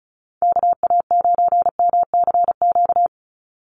Morse code, heard as a keyed tone, about 34 wpm, 700 Hz. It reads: KA9MCQ
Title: USA Call Signs 35wpm